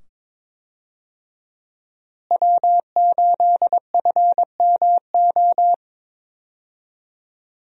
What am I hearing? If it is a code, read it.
Morse code, heard as a keyed tone, about 22 wpm, 700 Hz. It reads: W8FMO
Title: USA Call Signs - 5 Characters - Repeated 2x - 22wpm